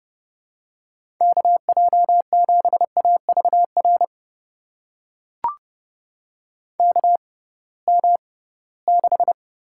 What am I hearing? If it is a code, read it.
Morse code, heard as a keyed tone, about 30 wpm, spaced as for 19 wpm, 700 Hz. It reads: KJ7AVR KM6
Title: USA Call Signs - 6 Characters - 6x Character Spacing - 30wpm